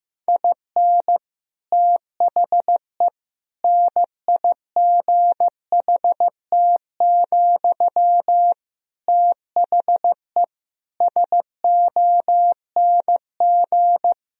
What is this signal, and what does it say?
Morse code, heard as a keyed tone, about 15 wpm, 700 Hz. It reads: IN THE NIGHT, THE SONG
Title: Sentences from Top 1000 Words - Review Part 2 - 15wpm